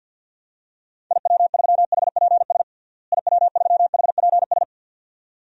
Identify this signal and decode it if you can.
Morse code, heard as a keyed tone, about 50 wpm, 700 Hz. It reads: IW3HPS IW3HPS
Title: DXCC-100 Call Signs - 6 Characters - Repeated 2x - 50wpm